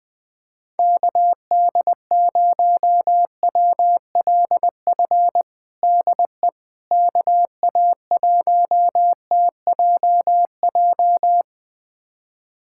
Morse code, heard as a keyed tone, about 20 wpm, 700 Hz. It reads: KD0WLF DE KA1TJJ
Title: USA Callsign DE Callsign - 6 Characters - 20wpm